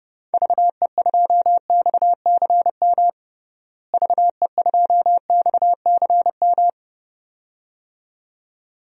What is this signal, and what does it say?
Morse code, heard as a keyed tone, about 30 wpm, 700 Hz. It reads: VE2XCM VE2XCM
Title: DXCC-100 Call Signs - 6 Characters - Repeated 2x - 30wpm